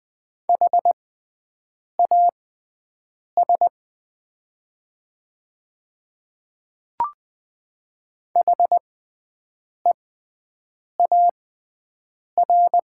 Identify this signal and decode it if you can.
Morse code, heard as a keyed tone, about 20 wpm, spaced as for 7 wpm, 700 Hz. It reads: HAS HEAR
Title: 6x Character Spacing - Top 300 Words - 20wpm